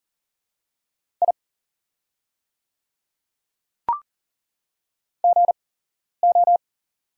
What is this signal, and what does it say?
Morse code, heard as a keyed tone, about 40 wpm, spaced as for 11 wpm, 700 Hz. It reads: I GO
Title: 8x Character Spacing - Top 200 Words - 40wpm